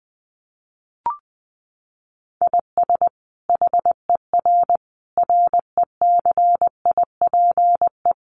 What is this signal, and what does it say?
Morse code, heard as a keyed tone, about 20 wpm, 700 Hz. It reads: IS HER RECIPE
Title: Sentences from Top 1200 Words 20wpm